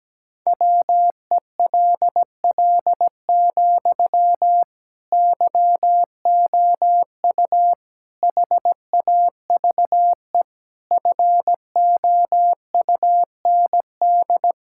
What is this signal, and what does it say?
Morse code, heard as a keyed tone, about 17 wpm, 700 Hz. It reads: WELL, YOU HAVE FOUND
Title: Sentences from Top 1000 Words - Review Part 6 - 17wpm